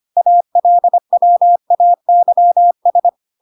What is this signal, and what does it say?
Morse code, heard as a keyed tone, about 25 wpm, 700 Hz. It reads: ALWAYS